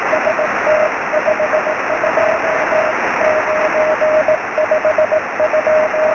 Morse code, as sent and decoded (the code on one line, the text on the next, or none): none